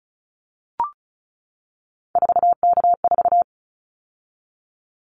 Morse code, heard as a keyed tone, about 36 wpm, 700 Hz. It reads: 4K4
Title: Callsign Prefixes for Top 100 DXCC - 35wpm